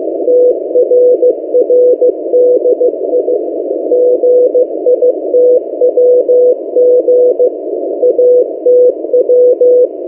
Morse code, heard as a keyed tone, about 15 wpm, 500 Hz. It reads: TRRDI GITWG ATW
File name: M08a- -1355utc-8096khz